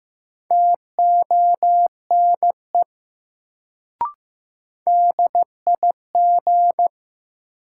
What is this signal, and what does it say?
Morse code, heard as a keyed tone, about 15 wpm, 700 Hz. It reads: TONE DIG
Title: Sentences from Top 1600 Words - New Word Drill Alt - 15wpm